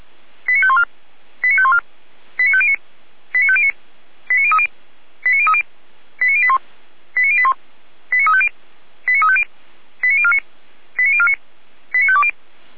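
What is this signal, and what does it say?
Five-tone selective call (ZVEI), heard as quick runs of five tones, tones of about 70 ms. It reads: 89514 89514 89509 89509 89030 89030 89081 89081 89350 89350 89049 89049 89520
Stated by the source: ZVEI five-tone selective call received on SDR.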